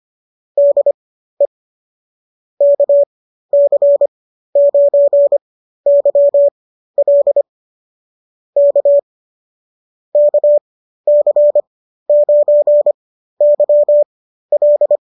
Morse code, heard as a keyed tone, about 25 wpm, spaced as for 13 wpm, 570 Hz. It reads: DE KC9YL K KC9YL